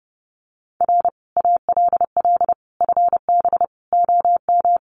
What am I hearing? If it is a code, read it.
Morse code, heard as a keyed tone, about 30 wpm, 700 Hz. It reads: R ALL FB OM
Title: Common QSO Phrases 30wpm